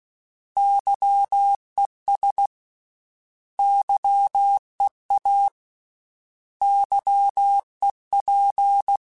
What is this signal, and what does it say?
Morse code, heard as a keyed tone, about 16 wpm, 800 Hz. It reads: YES YEA YEP
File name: Misc / Misc-mp3s / Misc @15